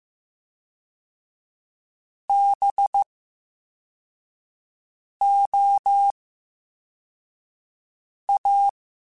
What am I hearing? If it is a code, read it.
Morse code, heard as a keyed tone, about 15 wpm, 800 Hz. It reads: B O A
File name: Special INGs / Special INGs-mp3s / Slow ing @13